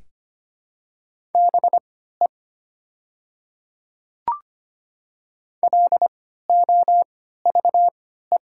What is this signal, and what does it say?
Morse code, heard as a keyed tone, about 25 wpm, spaced as for 14 wpm, 700 Hz. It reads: BE LOVE
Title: 3x Character Spacing - Top 500 Words - 25wpm